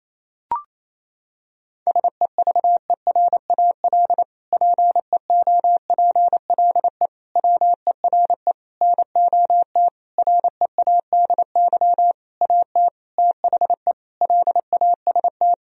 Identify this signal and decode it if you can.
Morse code, heard as a keyed tone, about 28 wpm, 700 Hz. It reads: SEVERAL PEOPLE WERE NOT READY AT THE LAST
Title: Sentences from Top 1000 Words - Review Part 2 - 28wpm